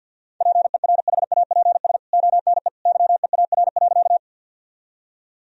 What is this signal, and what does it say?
Morse code, heard as a keyed tone, about 49 wpm, 700 Hz. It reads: PERHAPS ONE YEAR.
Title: Sentences from Top 1000 Words - Review Part 1 - 50wpm